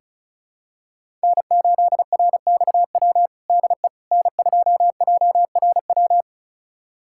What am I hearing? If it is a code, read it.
Morse code, heard as a keyed tone, about 34 wpm, 700 Hz. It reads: N8RXW DE N2JRW
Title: USA Callsign DE Callsign - 5 Characters - 35wpm